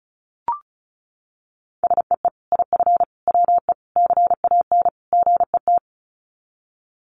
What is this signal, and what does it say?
Morse code, heard as a keyed tone, about 34 wpm, 700 Hz. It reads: SEE IF WE CAN GET
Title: Sets of 5 Words Encore2 35wpm